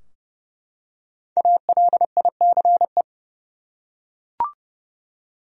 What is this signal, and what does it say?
Morse code, heard as a keyed tone, about 30 wpm, 700 Hz. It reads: ALICE
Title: USA Common Names - Max 5 Letters - 30wpm